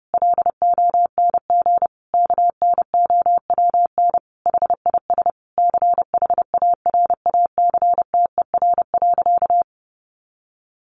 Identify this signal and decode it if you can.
Morse code, heard as a keyed tone, about 30 wpm, 700 Hz. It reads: LONG KNOWN HIS CHARACTER.
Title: Sentences from Top 1200 Words 30wpm